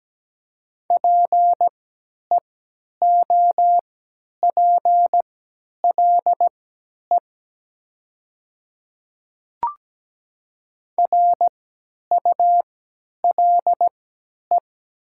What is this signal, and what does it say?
Morse code, heard as a keyed tone, about 17 wpm, spaced as for 10 wpm, 700 Hz. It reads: PEOPLE RULE
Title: 3x Character Spacing - Top 500 Words - 17wpm